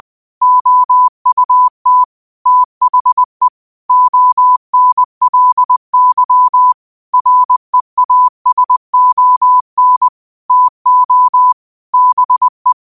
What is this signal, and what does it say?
Morse code, heard as a keyed tone, about 20 wpm, 1000 Hz. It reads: OUT THE ONLY REASON TO BE
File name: 2024-09-13-20wpm